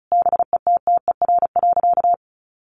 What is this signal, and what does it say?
Morse code, heard as a keyed tone, about 34 wpm, 700 Hz. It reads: BETTER.
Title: Sentences from Top 1100 Words 35wpm